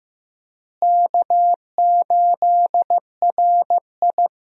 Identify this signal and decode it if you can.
Morse code, heard as a keyed tone, about 14 wpm, 700 Hz. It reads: K8RI
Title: USA Call Signs - 4 Characters - Repeated 3x - 15wpm